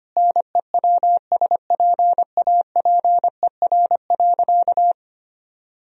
Morse code, heard as a keyed tone, about 25 wpm, 700 Hz. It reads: NEWSPAPER.
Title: Sentences from Top 1000 Words - Review Part 4 - 25wpm